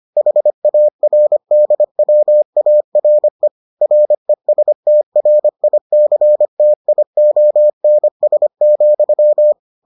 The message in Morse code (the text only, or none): HARDWARE RESTRICTIONS,